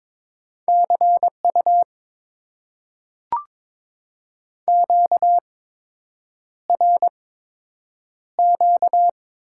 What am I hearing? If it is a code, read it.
Morse code, heard as a keyed tone, about 22 wpm, spaced as for 8 wpm, 690 Hz. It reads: CU QRQ
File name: Top 90 QSO Elements - 8x Character Spacing - 22wpm